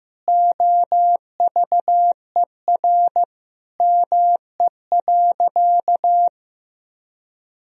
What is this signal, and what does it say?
Morse code, heard as a keyed tone, about 15 wpm, 700 Hz. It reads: OVER ME.
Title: Sentences from Top 1000 Words - Review Part 2 - 15wpm